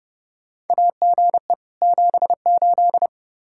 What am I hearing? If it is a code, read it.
Morse code, heard as a keyed tone, about 30 wpm, 700 Hz. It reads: AGE 78
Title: Practice QSO - Exchange 3 - More Info - 30wpm